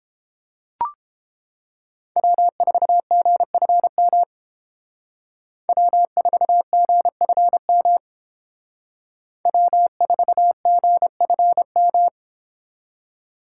Morse code, heard as a keyed tone, about 28 wpm, 700 Hz. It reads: W4GFM W4GFM W4GFM